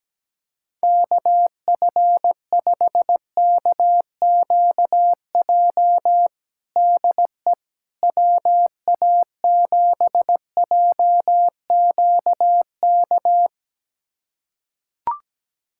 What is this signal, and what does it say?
Morse code, heard as a keyed tone, about 17 wpm, 700 Hz. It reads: KF5KQJ DE WA7JQK
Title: USA Callsign DE Callsign 17wpm